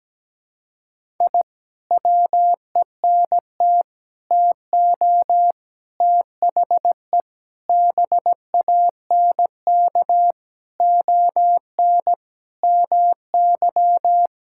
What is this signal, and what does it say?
Morse code, heard as a keyed tone, about 17 wpm, 700 Hz. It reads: I WENT TO THE BANK ON MY
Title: Sentences from Top 1000 Words - Review Part 6 - 17wpm